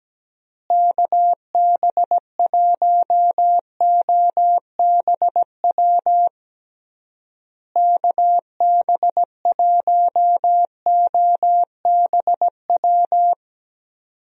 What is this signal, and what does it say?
Morse code, heard as a keyed tone, about 17 wpm, 700 Hz. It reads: KB1OBW KB1OBW
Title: USA Call Signs - 6 Characters - Repeated 3x - 17wpm